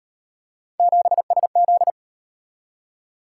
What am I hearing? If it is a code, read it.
Morse code, heard as a keyed tone, about 38 wpm, 700 Hz. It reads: 7SZ